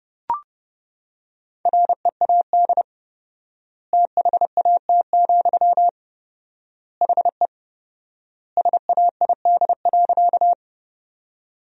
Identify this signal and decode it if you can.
Morse code, heard as a keyed tone, about 30 wpm, 700 Hz. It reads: READ THAT, HE SAID.
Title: 4x Word-Spacing - Sentences from Top 200 Words - 30wpm